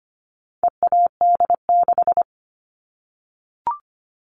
Morse code, heard as a keyed tone, about 25 wpm, 700 Hz. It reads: EAD6
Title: All Cut Numbers - 4-Digit 25wpm